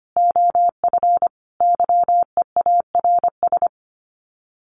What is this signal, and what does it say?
Morse code, heard as a keyed tone, about 25 wpm, 700 Hz. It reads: OF YEARS
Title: Sets of 5 Words Encore 25wpm